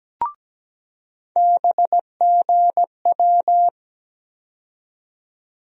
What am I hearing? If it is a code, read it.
Morse code, heard as a keyed tone, about 17 wpm, 700 Hz. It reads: BGW
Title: USA Call Sign Postfixes - 3 Letters 17wpm